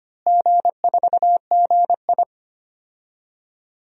Morse code, heard as a keyed tone, about 25 wpm, 700 Hz. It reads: G4GI
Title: DXCC-100 Call Signs - 4 Characters - Repeated 3x - 25wpm